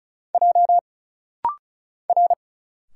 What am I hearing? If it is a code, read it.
Morse code, heard as a keyed tone, about 34 wpm, 700 Hz. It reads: J R
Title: Single Letters - Rapid-Fire 35wpm